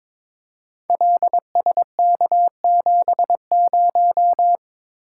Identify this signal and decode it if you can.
Morse code, heard as a keyed tone, about 23 wpm, 700 Hz. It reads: LSK70
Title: Code Groups - 5 Letter:Numbers - 22wpm